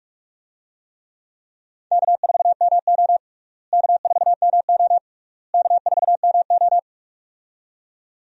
Morse code, heard as a keyed tone, about 46 wpm, 700 Hz. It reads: K4MO K4MO K4MO
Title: USA Call Signs - 4 Characters - Repeated 3x - 45wpm